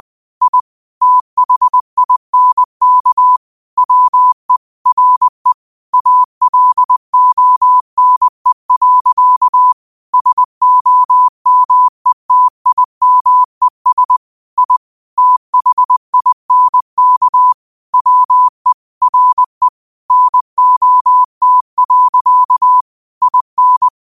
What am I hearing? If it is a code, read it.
Morse code, heard as a keyed tone, about 20 wpm, 1000 Hz. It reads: I THINK WE RE ALONE. SOMETIMES I THINK WE RE NOT. IN
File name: qotdfiles / 20wpm